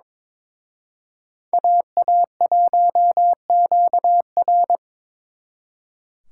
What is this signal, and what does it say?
Morse code, heard as a keyed tone, about 26 wpm, 700 Hz. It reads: AA1QR
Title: Speed Racing - USA 2x2 Call Signs 22wpm